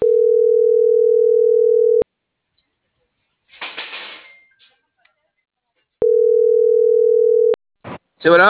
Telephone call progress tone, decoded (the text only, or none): ringback tone